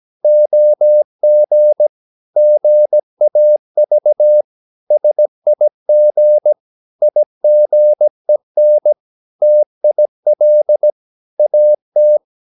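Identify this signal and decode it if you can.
Morse code, heard as a keyed tone, about 17 wpm, 600 Hz. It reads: OG GAV SIG IGEN TIL AT
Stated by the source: Slow (QRS) Morse practice at 17 wpm, from CW med Gnister.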